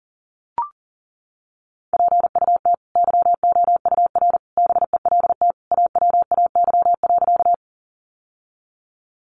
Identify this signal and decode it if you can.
Morse code, heard as a keyed tone, about 40 wpm, 700 Hz. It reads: PUT YOUR BELT AWAY.
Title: Sentences from Top 1400 Words 40wpm